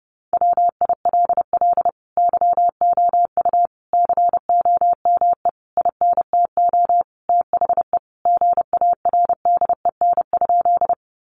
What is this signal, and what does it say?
Morse code, heard as a keyed tone, about 30 wpm, 700 Hz. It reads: WILL YOU COME INTO THE GARDEN?
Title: Sentences from Top 700 Words 30wpm